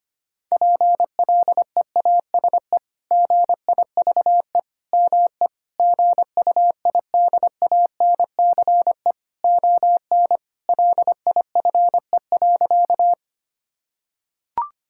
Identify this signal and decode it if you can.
Morse code, heard as a keyed tone, about 25 wpm, 700 Hz. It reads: PLEASE GIVE ME GUIDANCE ON LIFE.
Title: Sentences from Top 1600 Words 25wpm